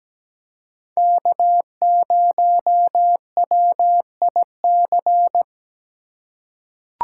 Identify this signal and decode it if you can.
Morse code, heard as a keyed tone, about 17 wpm, 700 Hz. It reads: K0WIC
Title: USA Call Signs - 5 Characters - 3x Character Spacing - 17wpm